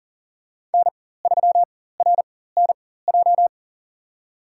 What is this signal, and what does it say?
Morse code, heard as a keyed tone, about 40 wpm, spaced as for 19 wpm, 700 Hz. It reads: N3RNJ
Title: USA Call Signs - 5 Characters - 4x Character Spacing - 40wpm